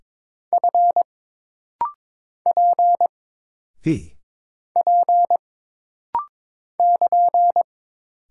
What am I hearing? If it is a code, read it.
Morse code, heard as a keyed tone, about 22 wpm, 700 Hz. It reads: F P P <KN>
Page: This rapid-fire timing is geared to push your ICR (Instant Character Recognition) skills.